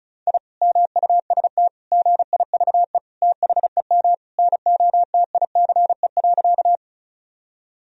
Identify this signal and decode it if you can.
Morse code, heard as a keyed tone, about 36 wpm, 700 Hz. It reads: I MUST GIVE THEM NOTICE.
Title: Sentences from Top 1000 Words - Review Part 2 - 35wpm